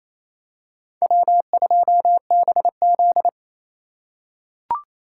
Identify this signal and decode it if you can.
Morse code, heard as a keyed tone, about 28 wpm, 700 Hz. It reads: W2BZ
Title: USA Call Signs - 4 Characters - 3x Character Spacing - 28wpm